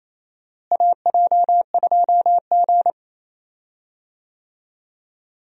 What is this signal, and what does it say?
Morse code, heard as a keyed tone, about 28 wpm, 700 Hz. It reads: AJ2G
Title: USA Call Signs - 4 Characters - Repeated 2x - 28wpm